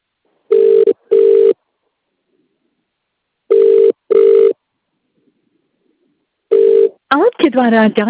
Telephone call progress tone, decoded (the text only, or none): ringback tone